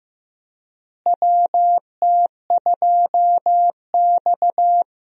Morse code, heard as a keyed tone, about 16 wpm, 700 Hz. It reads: WT2X
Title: USA Call Signs - 4 Characters - Repeated 2x - 15wpm